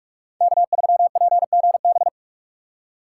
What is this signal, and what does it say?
Morse code, heard as a keyed tone, about 43 wpm, 700 Hz. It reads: K3PGB
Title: USA Call Signs - 5 Characters - 2x Character Spacing - 45wpm